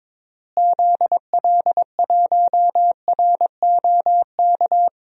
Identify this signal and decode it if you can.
Morse code, heard as a keyed tone, about 22 wpm, 700 Hz. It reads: ZL1ROK
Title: DXCC-100 Call Signs - 6 Characters - Repeated 3x - 22wpm